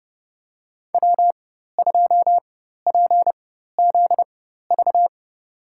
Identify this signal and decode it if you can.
Morse code, heard as a keyed tone, about 30 wpm, spaced as for 14 wpm, 700 Hz. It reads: W2PZV
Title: USA Call Signs - 5 Characters - 4x Character Spacing - 30wpm